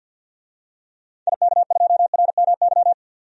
Morse code, heard as a keyed tone, about 46 wpm, 700 Hz. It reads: IK2RMY
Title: DXCC-100 Call Signs - 6 Characters - 50wpm